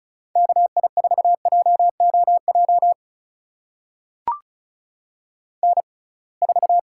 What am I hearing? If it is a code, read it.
Morse code, heard as a keyed tone, about 36 wpm, spaced as for 27 wpm, 700 Hz. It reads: KI4JOJ N4
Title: USA Call Signs - 6x Character Spacing - 35wpm